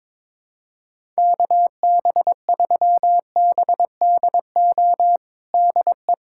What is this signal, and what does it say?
Morse code, heard as a keyed tone, about 22 wpm, 700 Hz. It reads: KB3BDO DE
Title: USA Callsign DE Callsign 22wpm